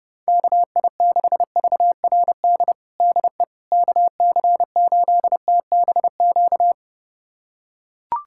Morse code, heard as a keyed tone, about 30 wpm, 700 Hz. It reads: KI6VRD DE KC8TBQ
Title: USA Callsign DE Callsign 30wpm